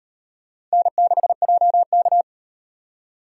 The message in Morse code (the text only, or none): N6JK